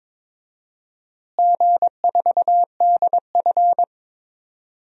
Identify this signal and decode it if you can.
Morse code, heard as a keyed tone, about 22 wpm, 700 Hz. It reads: G4DF
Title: DXCC-100 Call Signs - 4 Characters - Repeated 2x - 22wpm